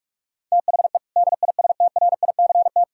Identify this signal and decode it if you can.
Morse code, heard as a keyed tone, about 46 wpm, 700 Hz. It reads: THE DISTRICT
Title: Sentences from Top 1600 Words 45wpm